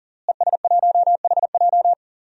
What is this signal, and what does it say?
Morse code, heard as a keyed tone, about 40 wpm, 700 Hz. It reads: ES1HJ
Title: DXCC-100 Call Signs - 5 Characters - Repeated 3x - 40wpm